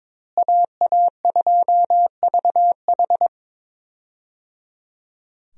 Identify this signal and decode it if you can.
Morse code, heard as a keyed tone, about 22 wpm, 700 Hz. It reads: AA2VH
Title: USA Call Signs - 5 Characters - Repeated 3x - 22wpm